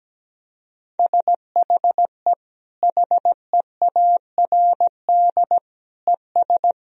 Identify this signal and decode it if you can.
Morse code, heard as a keyed tone, about 17 wpm, 700 Hz. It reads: SHE HEARD ES
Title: Sentences from Top 500 Words and Abbreviations 17wpm